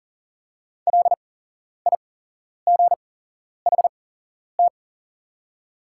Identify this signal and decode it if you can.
Morse code, heard as a keyed tone, about 40 wpm, spaced as for 11 wpm, 700 Hz. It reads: LIGHT
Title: 8x Character Spacing - Top 300 Words - 40wpm